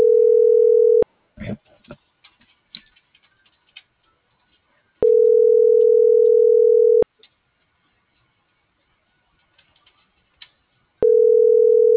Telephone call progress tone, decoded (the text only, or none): ringback tone